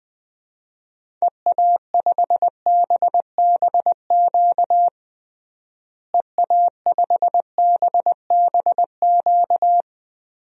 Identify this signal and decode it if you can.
Morse code, heard as a keyed tone, about 20 wpm, 700 Hz. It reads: EA5BBQ EA5BBQ
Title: DXCC-100 Call Signs - 6 Characters - Repeated 2x - 20wpm